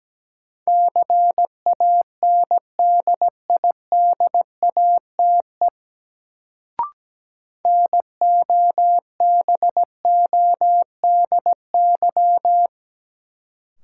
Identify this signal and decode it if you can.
Morse code, heard as a keyed tone, about 17 wpm, 700 Hz. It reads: CANDIDATE NOBODY
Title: Sentences from Top 1600 Words - New Word Drill - 17wpm